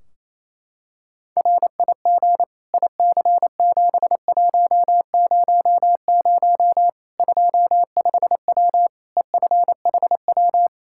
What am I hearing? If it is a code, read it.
Morse code, heard as a keyed tone, about 28 wpm, 700 Hz. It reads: RIG IC7100 25W EFHW
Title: Practice QSO - Exchange 2 - Info - 28wpm